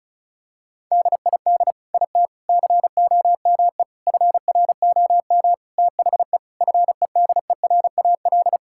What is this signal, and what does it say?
Morse code, heard as a keyed tone, about 36 wpm, 700 Hz. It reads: DID IT COME FROM THE FEDERAL
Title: Sentences from Top 1000 Words - Review Part 5 - 35wpm